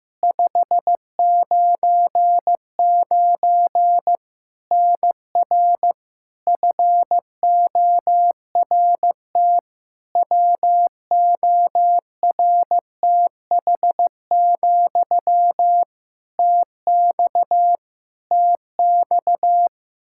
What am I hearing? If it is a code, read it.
Morse code, heard as a keyed tone, about 15 wpm, 700 Hz. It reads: 599 NR FORT WORTH, TX TX
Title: Practice QSO - Exchange 1 - Beginning - 15wpm